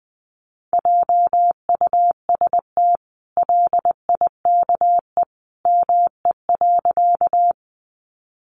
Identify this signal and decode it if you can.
Morse code, heard as a keyed tone, about 20 wpm, 700 Hz. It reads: JUST LIKE ME.
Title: Sentences from Top 200 Words 20wpm